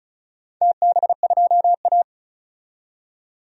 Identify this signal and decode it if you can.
Morse code, heard as a keyed tone, about 36 wpm, 700 Hz. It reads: TB2A
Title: DXCC-100 Call Signs - 4 Characters - 35wpm